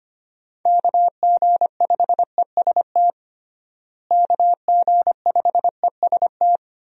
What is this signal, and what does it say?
Morse code, heard as a keyed tone, about 25 wpm, 700 Hz. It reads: KG5EST KG5EST
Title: USA Call Signs - 6 Characters - Repeated 3x - 25wpm